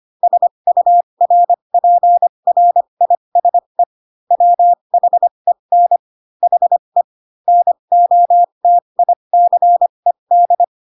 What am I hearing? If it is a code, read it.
Morse code, heard as a keyed tone, about 25 wpm, spaced as for 22 wpm, 700 Hz. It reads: SURPRISE WHEN HE NOTICED
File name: Pinocchio / Pinocchio ch3 18 25 ik1hge